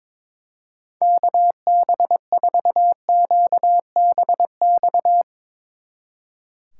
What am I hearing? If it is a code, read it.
Morse code, heard as a keyed tone, about 22 wpm, 700 Hz. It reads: KB4QBX
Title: Speed Racing - USA 2x3 Call Signs 22wpm